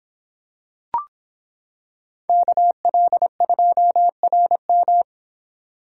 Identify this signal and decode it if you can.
Morse code, heard as a keyed tone, about 25 wpm, 700 Hz. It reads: KL2RM